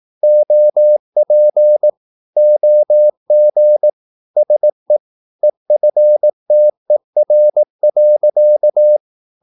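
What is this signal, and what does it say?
Morse code, heard as a keyed tone, about 18 wpm, 600 Hz. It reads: OP OG SE EFTER.
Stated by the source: Slow (QRS) Morse practice at 18 wpm, CW med Gnister.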